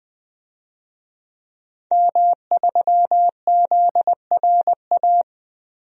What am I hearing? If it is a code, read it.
Morse code, heard as a keyed tone, about 20 wpm, 700 Hz. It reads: M3ZRA
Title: DXCC-100 Call Signs - 5 Characters - Repeated 2x - 20wpm